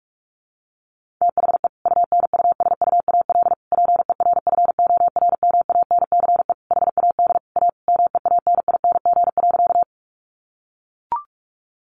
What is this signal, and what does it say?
Morse code, heard as a keyed tone, about 44 wpm, 700 Hz. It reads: THE UNUSUAL PERFORMANCE HAD A MEANING.
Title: Sentences from Top 1400 Words 45wpm